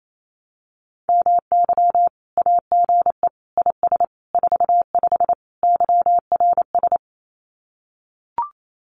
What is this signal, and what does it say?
Morse code, heard as a keyed tone, about 28 wpm, 700 Hz. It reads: MY AGE IS 45 YRS
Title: Common QSO Phrases Encore 28wpm